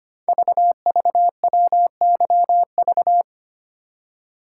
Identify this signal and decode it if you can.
Morse code, heard as a keyed tone, about 25 wpm, 700 Hz. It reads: VVWYV
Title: Code Groups - 5 Letters - 25wpm